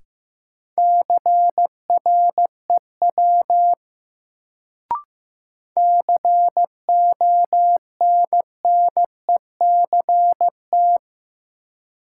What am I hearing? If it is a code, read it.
Morse code, heard as a keyed tone, about 15 wpm, 700 Hz. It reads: CREW CONNECT
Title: Sentences from Top 1500 Words - New Word Drill - 15wpm